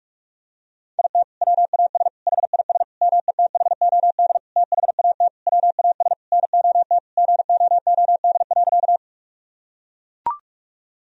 Morse code, heard as a keyed tone, about 41 wpm, 700 Hz. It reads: IT WAS HIS METHOD THAT WAS NOT GOOD.
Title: Sentences from Top 1000 Words - Review Part 3 - 45wpm